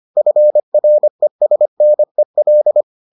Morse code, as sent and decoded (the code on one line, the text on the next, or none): ..-. .-. ... -. .-..
FRESNEL